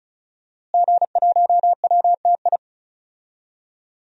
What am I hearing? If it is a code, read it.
Morse code, heard as a keyed tone, about 34 wpm, 700 Hz. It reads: G1WTI
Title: DXCC-100 Call Signs - 5 Characters - Repeated 2x - 35wpm